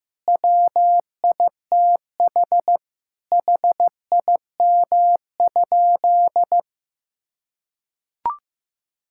Morse code, heard as a keyed tone, about 15 wpm, 700 Hz. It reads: WITH HIM?
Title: Sentences from Top 1000 Words - Review Part 5 - 15wpm